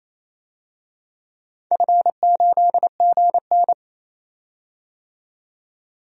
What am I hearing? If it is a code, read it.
Morse code, heard as a keyed tone, about 28 wpm, 700 Hz. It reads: F8GN
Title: DXCC-100 Call Signs - 4 Characters - 28wpm